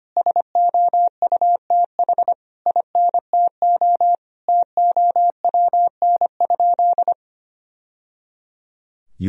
Morse code, heard as a keyed tone, about 25 wpm, 700 Hz. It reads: SOUTH INTO TOWN?
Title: Sentences from Top 1000 Words - Review Part 2 - 25wpm